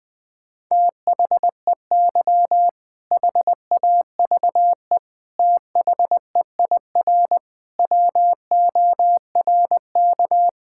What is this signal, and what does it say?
Morse code, heard as a keyed tone, about 20 wpm, 700 Hz. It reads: THEY HAVE THEIR WORK
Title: Sentences with 7-Word Phrases - Spoken Beginning - 20wpm